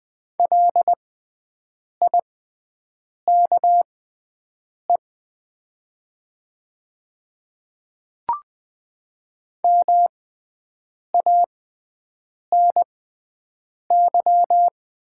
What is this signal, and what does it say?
Morse code, heard as a keyed tone, about 20 wpm, spaced as for 7 wpm, 700 Hz. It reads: LIKE MANY
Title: 6x Character Spacing - Top 100 Words - 20wpm